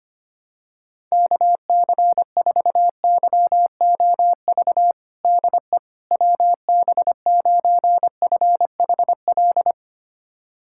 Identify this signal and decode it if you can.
Morse code, heard as a keyed tone, about 25 wpm, 700 Hz. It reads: KC4YOV DE WB9FHL
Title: USA Callsign DE Callsign - 6 Characters - 25wpm